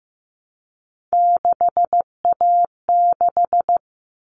Morse code, heard as a keyed tone, about 15 wpm, 700 Hz. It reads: 6A6
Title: All Cut Numbers - 3-Digit 15wpm